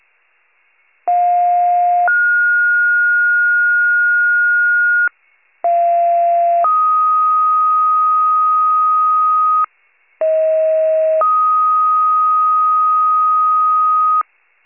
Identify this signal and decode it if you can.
Two-tone page, heard as a one-second tone, then a three-second tone, about 710 then 1430 Hz, about 690 then 1250 Hz, about 640 then 1250 Hz. It is heard from 1.1 to 14.2 s.
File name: pager / 46 FIRE / wcalert